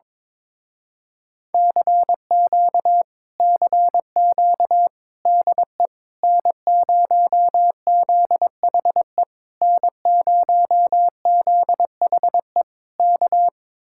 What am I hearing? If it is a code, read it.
Morse code, heard as a keyed tone, about 22 wpm, 700 Hz. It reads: CQ CQ DE N0ZHE N0ZHE K
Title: Speed-Racing Common QSO Phrases - 22wpm